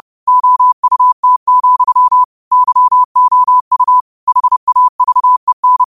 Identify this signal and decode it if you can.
Morse code, heard as a keyed tone, about 30 wpm, 1000 Hz. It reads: OUT, YOU HAVEN
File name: qotdfiles / 30wpm